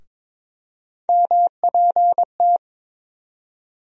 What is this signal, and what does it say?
Morse code, heard as a keyed tone, about 22 wpm, 700 Hz. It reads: MPT
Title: USA Call Sign Postfixes - 3 Letters 22wpm